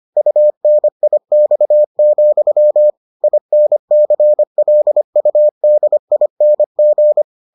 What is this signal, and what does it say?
Morse code, heard as a keyed tone, about 25 wpm, 600 Hz. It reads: UNIX, INCLUDING